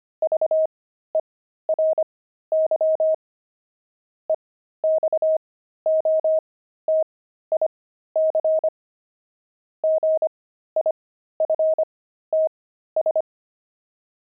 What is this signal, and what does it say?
Morse code, heard as a keyed tone, about 25 wpm, spaced as for 13 wpm, 630 Hz. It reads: VERY EXOTIC GIFTS
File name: pangram-fs-13-cs-25-jerks